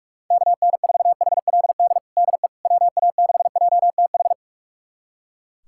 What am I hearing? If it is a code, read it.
Morse code, heard as a keyed tone, about 43 wpm, 700 Hz. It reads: KN4HLD DE WA6JTH
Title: USA Callsign DE Callsign 45wpm